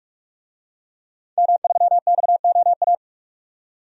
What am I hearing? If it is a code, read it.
Morse code, heard as a keyed tone, about 46 wpm, 700 Hz. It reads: M3XOA
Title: DXCC-100 Call Signs - 5 Characters - 45wpm